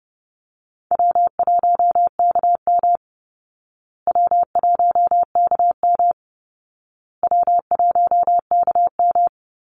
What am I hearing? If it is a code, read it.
Morse code, heard as a keyed tone, about 30 wpm, 700 Hz. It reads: W1KM W1KM W1KM
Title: Easy Call Signs Repeated 3x - 30wpm